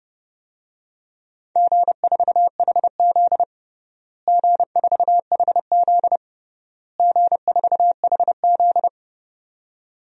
Morse code, heard as a keyed tone, about 30 wpm, 700 Hz. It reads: G4HZ G4HZ G4HZ
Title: DXCC-100 Call Signs - 4 Characters - Repeated 3x - 30wpm